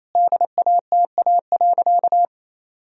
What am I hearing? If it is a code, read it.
Morse code, heard as a keyed tone, about 28 wpm, 700 Hz. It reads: DATA.
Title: Sentences from Top 1000 Words - Review Part 3 - 28wpm